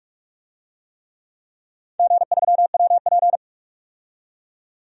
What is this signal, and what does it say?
Morse code, heard as a keyed tone, about 46 wpm, 700 Hz. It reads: G3WP
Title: DXCC-100 Call Signs - 4 Characters - 45wpm